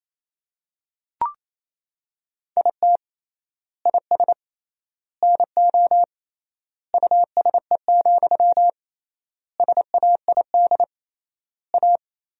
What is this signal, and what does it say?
Morse code, heard as a keyed tone, about 28 wpm, 700 Hz. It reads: IT IS NO USE, SAID A
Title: 3x Word-Spacing - Sentences from Top 500 Words - 28wpm